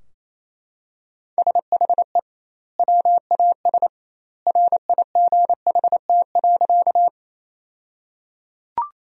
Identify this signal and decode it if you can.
Morse code, heard as a keyed tone, about 27 wpm, 700 Hz. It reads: SHE WAS RIGHT.
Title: 2x Word-Spacing - Sentences from Top 200 Words - 28wpm